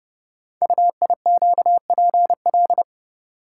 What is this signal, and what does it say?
Morse code, heard as a keyed tone, about 30 wpm, 700 Hz. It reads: UIQPL
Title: Code Groups - 5 Letters - 30wpm